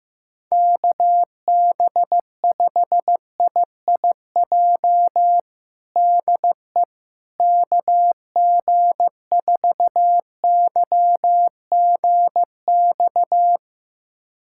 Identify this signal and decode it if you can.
Morse code, heard as a keyed tone, about 15 wpm, 700 Hz. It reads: KB5IIJ DE KG4YGX
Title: USA Callsign DE Callsign - 6 Characters - 15wpm